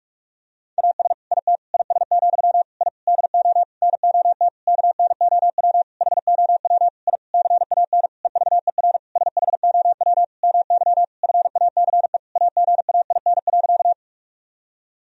Morse code, heard as a keyed tone, about 43 wpm, 700 Hz. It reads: AS IT IS, I DO NOT KNOW HOW I CAN EVER SHOW MY FACE AGAIN.
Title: Sentences from Top 1000 Words - Review Part 2 - 45wpm